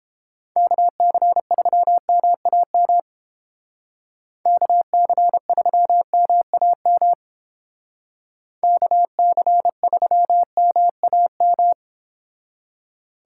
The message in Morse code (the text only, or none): KC3MAM KC3MAM KC3MAM